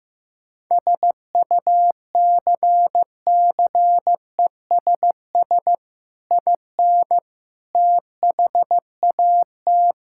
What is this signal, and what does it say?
Morse code, heard as a keyed tone, about 15 wpm, 700 Hz. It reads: SUCCESS IN THAT
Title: Sentences from Top 1000 Words - Review Part 5 - 15wpm